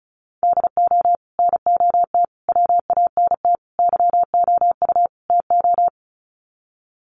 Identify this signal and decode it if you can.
Morse code, heard as a keyed tone, about 34 wpm, 700 Hz. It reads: DO NOT WANT YOU TO
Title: Sets of 5 Words 35wpm